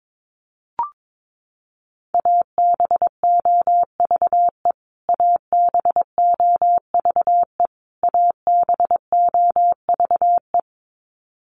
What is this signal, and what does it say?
Morse code, heard as a keyed tone, about 22 wpm, 700 Hz. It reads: ABOVE ABOVE ABOVE
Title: Top 500 Words Repeated 22wpm